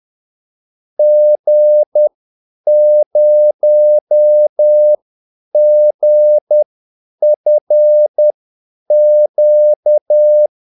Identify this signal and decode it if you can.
Morse code, heard as a keyed tone, about 10 wpm, spaced as for 8 wpm, 600 Hz. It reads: G0GFQ
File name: Callsigns-10-8wpm